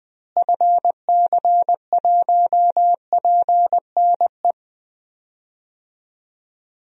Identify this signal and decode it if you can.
Morse code, heard as a keyed tone, about 21 wpm, 700 Hz. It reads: FC1PNE
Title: DXCC-100 Call Signs - 6 Characters - Repeated 2x - 20wpm